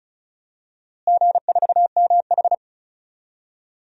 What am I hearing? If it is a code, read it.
Morse code, heard as a keyed tone, about 36 wpm, 700 Hz. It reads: G4MH
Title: DXCC-100 Call Signs - 4 Characters - Repeated 3x - 35wpm